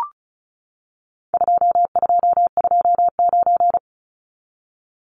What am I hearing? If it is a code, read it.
Morse code, heard as a keyed tone, about 34 wpm, 700 Hz. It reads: 2229
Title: Numbers - 4-Digit 35wpm